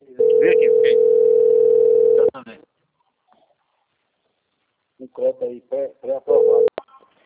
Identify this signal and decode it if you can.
Ringback tone (North America), 440 + 480 Hz, with speech.
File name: out- - - - - .1827